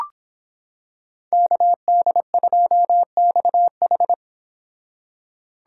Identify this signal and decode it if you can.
Morse code, heard as a keyed tone, about 26 wpm, 700 Hz. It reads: KD2XH